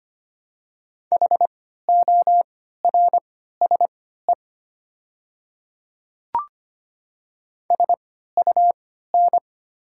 Morse code, heard as a keyed tone, about 25 wpm, spaced as for 14 wpm, 700 Hz. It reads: HORSE SUN
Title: 3x Character Spacing - Top 300 Words - 25wpm